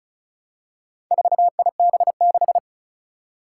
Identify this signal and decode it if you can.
Morse code, heard as a keyed tone, about 34 wpm, 700 Hz. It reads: 4IB6
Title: Four Challenging Letter-Number Combinations 35wpm